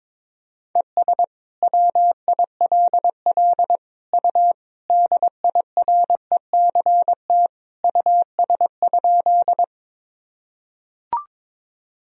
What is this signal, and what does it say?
Morse code, heard as a keyed tone, about 22 wpm, 700 Hz. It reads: ES WILL U DIRECT US?
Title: Sentences from Top 400 Words and Abbreviations 22wpm